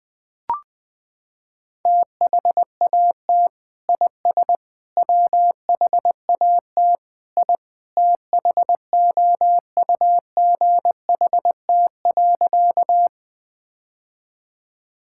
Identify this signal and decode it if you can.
Morse code, heard as a keyed tone, about 20 wpm, 700 Hz. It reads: THAT IS WHAT I THOUGHT.
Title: Sentences with 2-Word Phrases - 20wpm